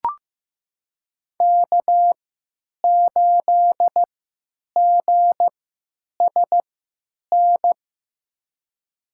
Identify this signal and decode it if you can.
Morse code, heard as a keyed tone, about 15 wpm, spaced as for 9 wpm, 700 Hz. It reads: K8GSN
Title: USA Call Signs - 3x Character Spacing - 15wpm